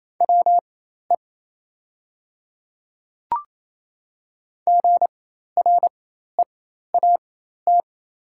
Morse code, heard as a keyed tone, about 28 wpm, spaced as for 13 wpm, 700 Hz. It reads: WE GREAT
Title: 4x Character Spacing - Top 200 Words - 28wpm